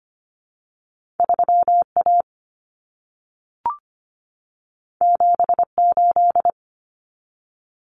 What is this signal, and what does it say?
Morse code, heard as a keyed tone, about 25 wpm, 700 Hz. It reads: 3 A 7 8
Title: Common Cut Numbers - 2-Digit 25wpm